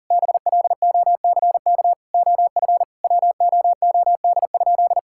Morse code, heard as a keyed tone, about 40 wpm, 700 Hz. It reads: BLOCK OF WOOD?
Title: Sentences from Top 1600 Words 40wpm